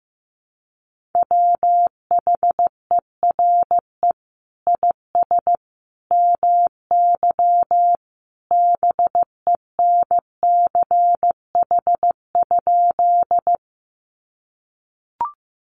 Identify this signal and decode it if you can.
Morse code, heard as a keyed tone, about 15 wpm, 700 Hz. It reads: WHERE IS MY BENCH?
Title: Sentences from Top 1400 Words 15wpm